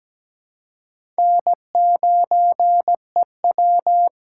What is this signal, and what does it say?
Morse code, heard as a keyed tone, about 17 wpm, 700 Hz. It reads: N9EW
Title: USA Call Signs - 4 Characters - 2x Character Spacing - 17wpm